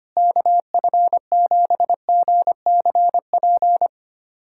Morse code, heard as a keyed tone, about 25 wpm, 700 Hz. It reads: KF7GCP
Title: USA Callsign DE Callsign - 6 Characters - 25wpm